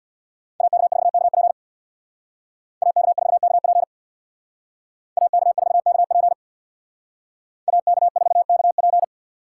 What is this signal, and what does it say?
Morse code, heard as a keyed tone, about 64 wpm, 690 Hz. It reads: AK4KP AK4KP AK4KP AK4KP